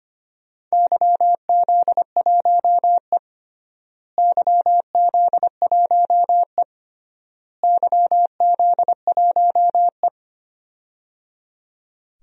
Morse code, heard as a keyed tone, about 25 wpm, 700 Hz. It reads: YZ1E YZ1E YZ1E
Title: DXCC-100 Call Signs - 4 Characters - Repeated 3x - 25wpm